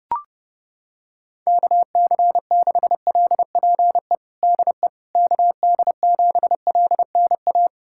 Morse code, heard as a keyed tone, about 30 wpm, 690 Hz. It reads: KC6LPE DE KD7LNA
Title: USA Callsign DE Callsign - 6 Characters - 30wpm